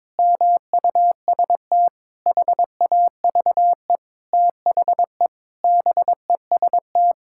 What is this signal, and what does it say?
Morse code, heard as a keyed tone, about 22 wpm, 700 Hz. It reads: MUST HAVE THE BEST
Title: Sentences from Top 1000 Words - Review Part 6 - 22wpm